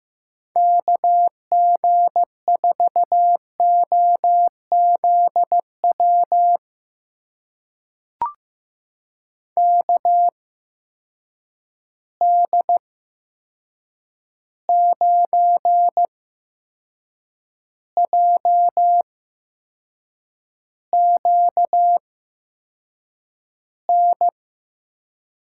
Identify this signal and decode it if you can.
Morse code, heard as a keyed tone, about 15 wpm, spaced as for 6 wpm, 700 Hz. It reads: KG4OZW KD9JQN
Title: USA Call Signs - 8x Character Spacing - 15wpm